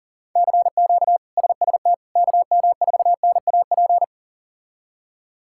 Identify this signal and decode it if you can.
Morse code, heard as a keyed tone, about 40 wpm, 700 Hz. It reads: CQ SST KM4NAP
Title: K1USN SST QSO Practice - 40wpm